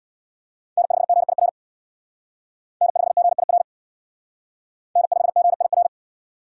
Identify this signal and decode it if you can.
Morse code, heard as a keyed tone, about 57 wpm, 690 Hz. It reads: N5GIR N5GIR N5GIR
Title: Speed Racing - USA 1x3 Call Signs 50wpm